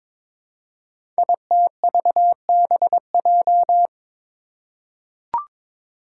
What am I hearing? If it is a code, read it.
Morse code, heard as a keyed tone, about 22 wpm, 700 Hz. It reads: ITVBJ
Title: Code Groups - 5 Letters - 22wpm